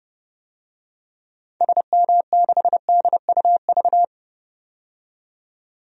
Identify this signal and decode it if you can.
Morse code, heard as a keyed tone, about 30 wpm, 700 Hz. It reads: SM6DUV
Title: DXCC-100 Call Signs - 6 Characters - 30wpm